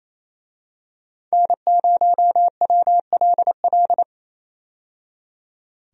Morse code, heard as a keyed tone, about 28 wpm, 700 Hz. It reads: N0WLL
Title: USA Call Signs - 5 Characters - 28wpm